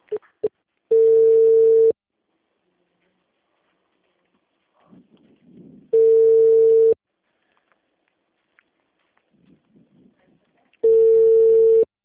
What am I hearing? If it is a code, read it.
Ringback tone (China, 450 Hz), with speech.